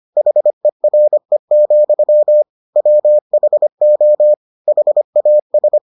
Morse code, heard as a keyed tone, about 25 wpm, 600 Hz. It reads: HERE, WHO HAS